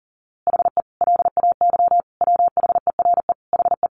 Morse code, heard as a keyed tone, about 40 wpm, 700 Hz. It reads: HE LAY WHERE HE
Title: A Sound Of Thunder 40wpm